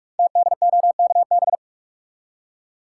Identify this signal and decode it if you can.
Morse code, heard as a keyed tone, about 44 wpm, 690 Hz. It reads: TDOKB
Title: Code Groups - 5 Letters - 45wpm